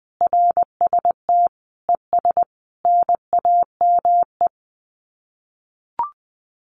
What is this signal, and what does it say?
Morse code, heard as a keyed tone, about 20 wpm, 700 Hz. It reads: RST ES NAME
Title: Common QSO Phrases 20wpm